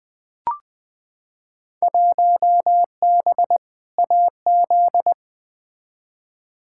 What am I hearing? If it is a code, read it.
Morse code, heard as a keyed tone, about 21 wpm, 700 Hz. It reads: 1B AZ
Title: Field Day Exchanges - 20wpm